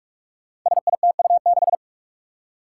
Morse code, heard as a keyed tone, about 46 wpm, 700 Hz. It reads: SITU6
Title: Five Challenging Letter-Number Combinations 45wpm